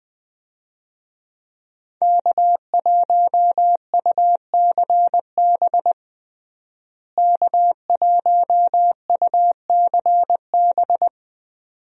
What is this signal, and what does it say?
Morse code, heard as a keyed tone, about 20 wpm, 700 Hz. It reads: K1UCB K1UCB
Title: USA Call Signs - 5 Characters - Repeated 2x - 20wpm